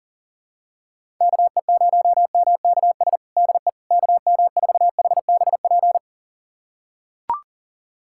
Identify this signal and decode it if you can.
Morse code, heard as a keyed tone, about 40 wpm, 700 Hz. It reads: KE0MKS DE KM4HBP
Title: USA Callsign DE Callsign - 6 Characters - 40wpm